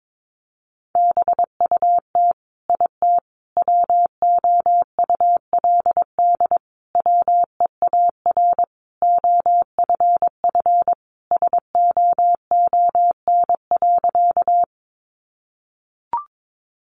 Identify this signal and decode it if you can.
Morse code, heard as a keyed tone, about 22 wpm, 700 Hz. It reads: BUT IT WOULD WEAR OFF SOON.
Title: Sentences from Top 1500 Words 22wpm